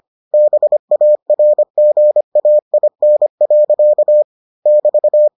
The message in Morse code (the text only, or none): BARGAIN. <BT>